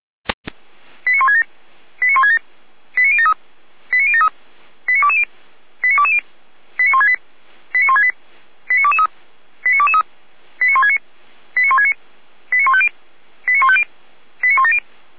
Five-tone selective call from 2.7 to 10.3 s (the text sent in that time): 89073 89073 89220 89220 89168 89168 89223 89223